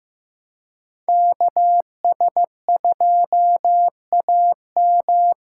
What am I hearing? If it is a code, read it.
Morse code, heard as a keyed tone, about 18 wpm, 700 Hz. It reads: KS2AM
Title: Speed Racing - USA 2x2 Call Signs 15wpm